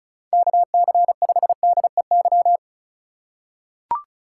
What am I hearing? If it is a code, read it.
Morse code, heard as a keyed tone, about 36 wpm, 700 Hz. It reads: KC5DEY
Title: USA Call Signs - 6 Characters - 8x Character Spacing - 35wpm